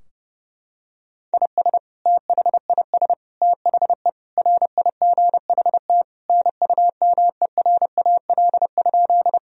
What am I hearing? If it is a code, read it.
Morse code, heard as a keyed tone, about 30 wpm, 700 Hz. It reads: IS THIS THE RIGHT NUMERAL?
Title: Sentences from Top 1000 Words - Review Part 2 - 30wpm